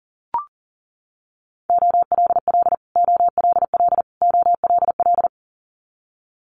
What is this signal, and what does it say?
Morse code, heard as a keyed tone, about 40 wpm, 700 Hz. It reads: OLL OLL OLL
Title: Multi-letter Phonemes Repeated 3x 40wpm